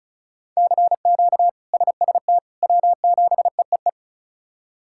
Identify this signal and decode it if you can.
Morse code, heard as a keyed tone, about 36 wpm, 700 Hz. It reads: CQ SST W7EEE
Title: K1USN SST QSO Practice - 35wpm